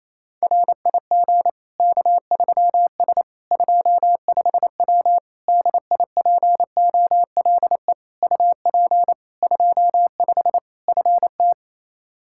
Morse code, heard as a keyed tone, about 28 wpm, 700 Hz. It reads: RIG K3S 25W DIPOLE UP 25 FT
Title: K1USN SST QSO Practice 28wpm